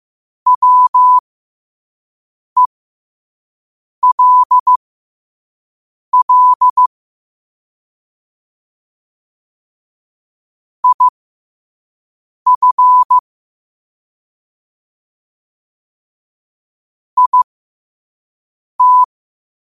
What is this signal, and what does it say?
Morse code, heard as a keyed tone, about 15 wpm, spaced as for 5 wpm, 1000 Hz. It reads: WELL IF IT
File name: qotdfiles / 5wpm